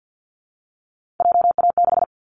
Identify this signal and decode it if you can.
Morse code, heard as a keyed tone, about 50 wpm, 700 Hz. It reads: JA6
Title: Callsign Prefixes for Top 25 DXCC - 50wpm